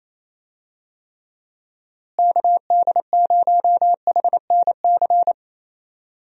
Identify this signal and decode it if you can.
Morse code, heard as a keyed tone, about 28 wpm, 700 Hz. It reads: KD0HNC
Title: USA Call Signs - 6 Characters - Repeated 2x - 28wpm